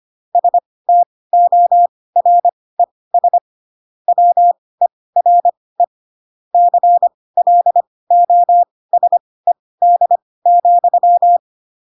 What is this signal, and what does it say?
Morse code, heard as a keyed tone, about 25 wpm, spaced as for 18 wpm, 700 Hz. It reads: STORES WERE CLOSED,
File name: Pinocchio / Pinocchio ch6 12 25 ik1hge